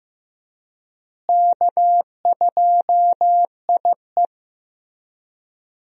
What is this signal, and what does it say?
Morse code, heard as a keyed tone, about 15 wpm, 700 Hz. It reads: K2IE
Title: USA Call Signs - 4 Characters - Repeated 3x - 15wpm